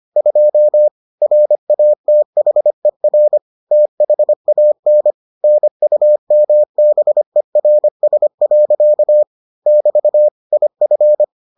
Morse code, heard as a keyed tone, about 25 wpm, 600 Hz. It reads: 2 RATHER THAN NUMBERS. <BT> IF